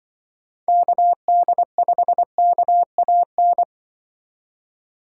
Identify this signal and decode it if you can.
Morse code, heard as a keyed tone, about 24 wpm, 700 Hz. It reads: KD5KAN
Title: Speed Racing - USA Call Signs 20wpm